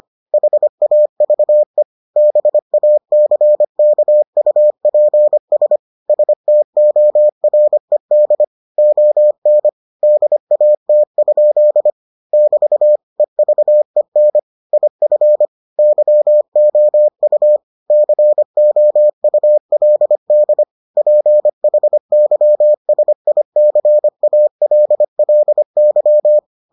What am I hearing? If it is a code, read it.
Morse code, heard as a keyed tone, about 25 wpm, 600 Hz. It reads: HAVE BACKUPS STORED ON DAT? <BT> EVEN IF YOU COULD PHYSICALLY